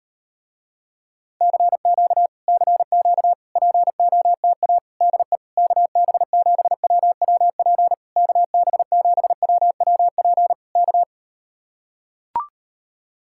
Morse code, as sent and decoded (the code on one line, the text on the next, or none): -.-. --.- / -.-. --.- / .--. --- - .- / -.. / -.- -... --... .-- .-- .--. / -.- -... --... .-- .-- .--. / -.-
CQ CQ POTA DE KB7WWP KB7WWP K